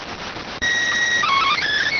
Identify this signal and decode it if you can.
SSTV (slow-scan television), PD 290.